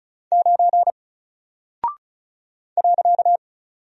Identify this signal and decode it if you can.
Morse code, heard as a keyed tone, about 34 wpm, 700 Hz. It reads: 9 .
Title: Single Character (Letter:Number:Punctuation:Prosign) Alt 35wpm